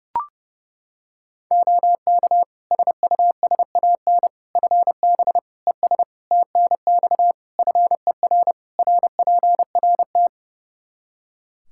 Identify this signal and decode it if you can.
Morse code, heard as a keyed tone, about 30 wpm, 700 Hz. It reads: OK SUSAN FB ES TNX FER RPRT
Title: Practice QSO - Exchange 2 - Info - 30wpm